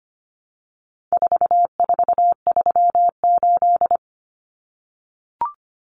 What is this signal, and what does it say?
Morse code, heard as a keyed tone, about 25 wpm, 700 Hz. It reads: 4438
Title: Numbers - 4-Digit 25wpm